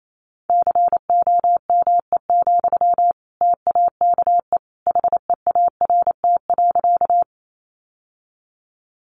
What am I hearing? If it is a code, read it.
Morse code, heard as a keyed tone, about 28 wpm, 700 Hz. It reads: COME, TAKE HEART.
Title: Sentences from Top 1200 Words 28wpm